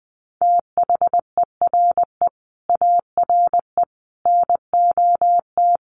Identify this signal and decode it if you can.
Morse code, heard as a keyed tone, about 20 wpm, 700 Hz. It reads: THERE ARE NOT
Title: Sentences from Top 1300 Words 20wpm